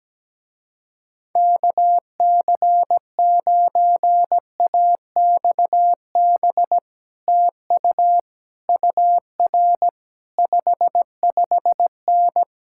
Morse code, heard as a keyed tone, about 17 wpm, 700 Hz. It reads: KC9AXB TU UR 55N
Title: Speed-Racing Common QSO Phrases - 17wpm